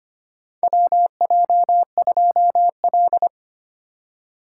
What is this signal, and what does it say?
Morse code, heard as a keyed tone, about 25 wpm, 700 Hz. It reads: WJ2L
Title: K1USN SST QSO Practice - 25wpm